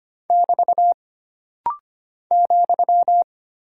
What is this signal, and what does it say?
Morse code, heard as a keyed tone, about 25 wpm, 700 Hz. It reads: <BT> ,
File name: Single Punctuation-Prosign - Rapid-Fire - 25wpm